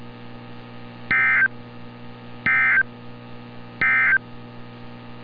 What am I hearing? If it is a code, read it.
Emergency Alert System (SAME) tones, heard as harsh buzzing bursts. It is heard from 1.0 to 4.3 s.
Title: NOAA Weather Radio - Altoona, PA